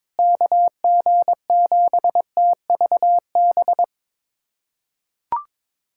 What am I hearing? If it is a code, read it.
Morse code, heard as a keyed tone, about 22 wpm, 700 Hz. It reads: KG7TVB
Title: USA Callsign DE Callsign 22wpm